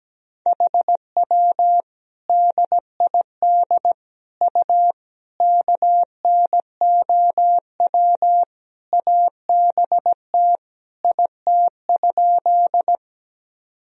Morse code, heard as a keyed tone, about 17 wpm, 700 Hz. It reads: HW DID U KNOW ABT IT?
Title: Sentences from Top 100 Words and Abbreviations 17wpm